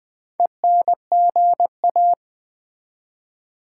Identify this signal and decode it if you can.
Morse code, heard as a keyed tone, about 20 wpm, 700 Hz. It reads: ENGA
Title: Build A Word - 2 Letters At A Time - 8-Letter Words - 20wpm